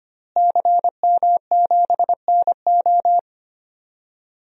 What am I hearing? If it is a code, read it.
Morse code, heard as a keyed tone, about 25 wpm, 700 Hz. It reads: CM7NO
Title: DXCC-100 Call Signs - 5 Characters - Repeated 3x - 25wpm